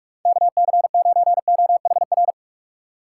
Morse code, heard as a keyed tone, about 47 wpm, 700 Hz. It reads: KC9OHR
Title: USA Call Signs - 6 Characters - 3x Character Spacing - 45wpm